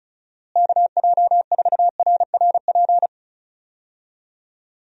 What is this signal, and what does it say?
Morse code, heard as a keyed tone, about 34 wpm, 700 Hz. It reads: KJ4RRP
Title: USA Call Signs - 6 Characters - Repeated 2x - 35wpm